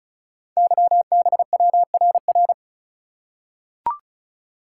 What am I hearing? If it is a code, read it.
Morse code, heard as a keyed tone, about 36 wpm, 690 Hz. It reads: YBWRR
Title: Code Groups - 5 Letters - 35wpm